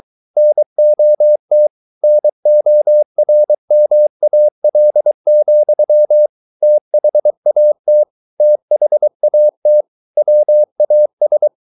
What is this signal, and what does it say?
Morse code, heard as a keyed tone, about 23 wpm, 600 Hz. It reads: NOT NORMAL, THAT THAT WAS